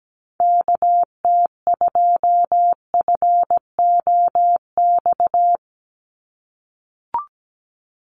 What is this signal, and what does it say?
Morse code, heard as a keyed tone, about 17 wpm, 700 Hz. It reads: KT2FOX
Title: USA Call Signs 17wpm